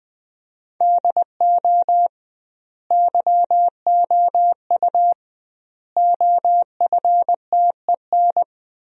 Morse code, heard as a keyed tone, about 20 wpm, 700 Hz. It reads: DO YOU OFTEN
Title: 2x Word-Spacing - Sentences from Top 300 Words - 20wpm